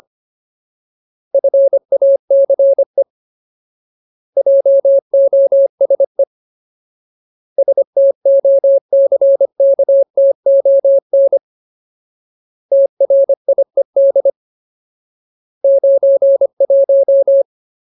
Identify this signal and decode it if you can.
Morse code, heard as a keyed tone, about 25 wpm, 550 Hz. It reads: FACE JOSE STOCKTON TRIED 91
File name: Ultimate Word List / 25WPM Ultimate